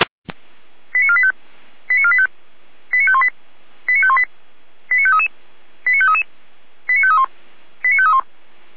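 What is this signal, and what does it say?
Five-tone selective call (ZVEI), heard as quick runs of five tones, tones of about 70 ms. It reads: 89485 89485 89518 89518 89533 89533 89531 89531
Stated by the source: Received on an SDR; ZVEI five-tone selective call.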